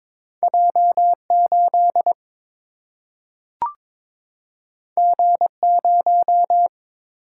Morse code, heard as a keyed tone, about 22 wpm, 700 Hz. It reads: J 8 G 0
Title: Single-Letter Single-Number 22wpm